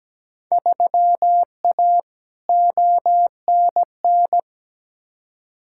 Morse code, heard as a keyed tone, about 17 wpm, 700 Hz. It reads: 3A ONN
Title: Field Day Exchanges Encore - 17wpm